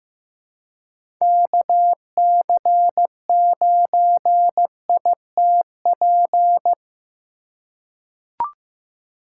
Morse code, heard as a keyed tone, about 15 wpm, 700 Hz. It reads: KC9ITP
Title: USA Call Signs - 3x Character Spacing - 15wpm